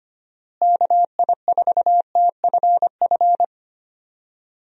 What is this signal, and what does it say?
Morse code, heard as a keyed tone, about 25 wpm, 700 Hz. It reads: KI4TFF
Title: USA Call Signs - 6 Characters - Repeated 3x - 25wpm